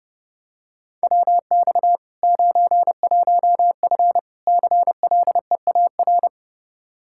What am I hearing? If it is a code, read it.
Morse code, heard as a keyed tone, about 32 wpm, 700 Hz. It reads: WX 91F CLEAR
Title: K1USN SST QSO Practice 30wpm